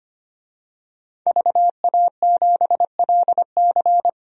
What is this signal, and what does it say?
Morse code, heard as a keyed tone, about 25 wpm, 700 Hz. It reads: VA7LC
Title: DXCC-100 Call Signs - 5 Characters - 25wpm